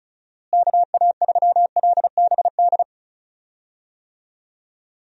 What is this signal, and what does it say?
Morse code, heard as a keyed tone, about 36 wpm, 700 Hz. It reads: KA3LBD
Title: USA Call Signs - 6 Characters - Repeated 2x - 35wpm